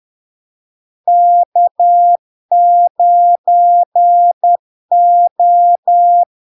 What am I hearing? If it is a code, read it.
Morse code, heard as a keyed tone, about 10 wpm, 700 Hz. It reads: K9O
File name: LARK-FD-2022-Log-in-CW-10wpm-700-Hz